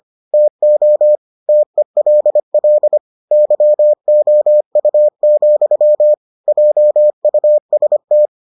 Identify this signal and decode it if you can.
Morse code, heard as a keyed tone, about 25 wpm, 600 Hz. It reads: TO TELL YOU, JUST